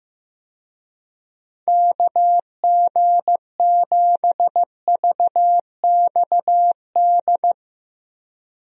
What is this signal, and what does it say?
Morse code, heard as a keyed tone, about 15 wpm, 700 Hz. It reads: KG7VXD
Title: USA Call Signs - 6 Characters - Repeated 3x - 15wpm